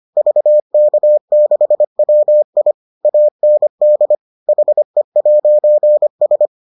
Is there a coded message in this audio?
Yes: Morse code from 0.2 to 6.5 s, heard as a keyed tone, about 25 wpm, 600 Hz. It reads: VK6WI AND HE'S